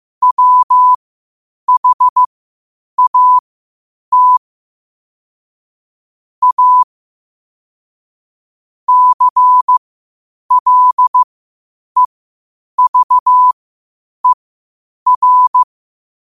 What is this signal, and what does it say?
Morse code, heard as a keyed tone, about 15 wpm, spaced as for 8 wpm, 1000 Hz. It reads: WHAT A CLEVER
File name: qotdfiles / 8wpm